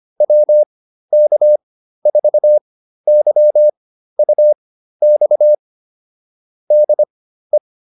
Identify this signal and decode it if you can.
Morse code, heard as a keyed tone, about 25 wpm, spaced as for 13 wpm, 600 Hz. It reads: WK4YUX DE
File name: QSO 305 13